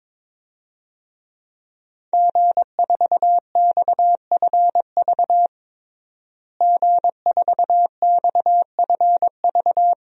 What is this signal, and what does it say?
Morse code, heard as a keyed tone, about 22 wpm, 700 Hz. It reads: G4XFV G4XFV
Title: DXCC-100 Call Signs - 5 Characters - Repeated 3x - 22wpm